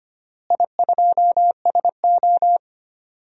Morse code, heard as a keyed tone, about 25 wpm, 700 Hz. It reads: I2SO
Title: DXCC-100 Call Signs - 4 Characters - Repeated 3x - 25wpm